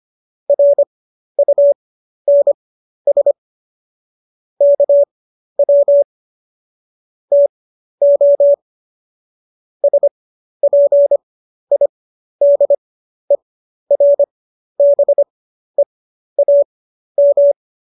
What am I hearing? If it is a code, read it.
Morse code, heard as a keyed tone, about 25 wpm, spaced as for 12 wpm, 570 Hz. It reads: RUNS KW TO SPIDERBEAM
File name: qso009 12